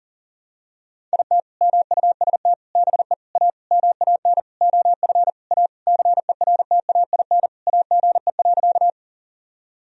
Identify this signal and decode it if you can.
Morse code, heard as a keyed tone, about 40 wpm, 700 Hz. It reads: IT MUST BE A MAN OF A CERTAIN AGE.
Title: Sentences from Top 1000 Words - Review Part 3 - 40wpm